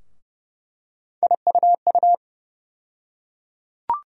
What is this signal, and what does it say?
Morse code, heard as a keyed tone, about 30 wpm, 700 Hz. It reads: IUU
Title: USA Call Sign Postfixes - 3 Letters 30wpm